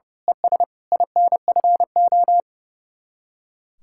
Morse code, heard as a keyed tone, about 29 wpm, 700 Hz. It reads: ES INFO
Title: Speed-Racing Common QSO Phrases - 30wpm